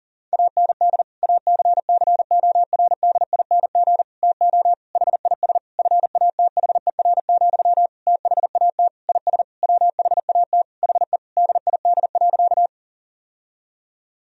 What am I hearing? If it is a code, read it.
Morse code, heard as a keyed tone, about 37 wpm, 700 Hz. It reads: AND ACCORDING TO HIS FATHER, THAT IS WHAT HE DID.
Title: Sentences from Top 1000 Words - Review Part 4 - 40wpm